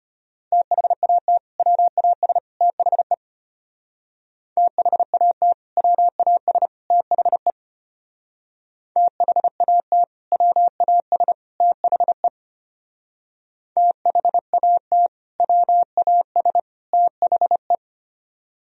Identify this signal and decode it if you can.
Morse code, heard as a keyed tone, about 32 wpm, 700 Hz. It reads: THAT WAS THE THAT WAS THE THAT WAS THE THAT WAS THE